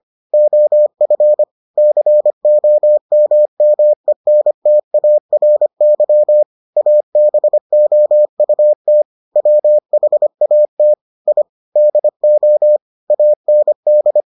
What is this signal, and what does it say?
Morse code, heard as a keyed tone, about 25 wpm, 600 Hz. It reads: OF COMMENTARY ABOUT WHAT I DO AND